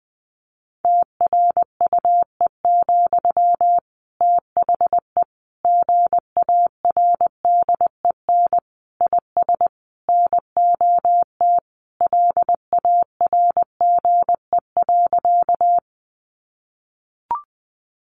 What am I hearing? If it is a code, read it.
Morse code, heard as a keyed tone, about 20 wpm, 700 Hz. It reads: TRUE, THE GARDEN IS NOT LARGE.
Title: Sentences from Top 700 Words 20wpm